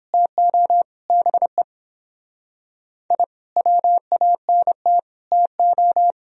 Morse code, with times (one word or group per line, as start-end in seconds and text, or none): 0.1-0.8 TO
1.1-1.6 BE
3.1-3.2 I
3.6-5.0 WANT
5.3-6.1 TO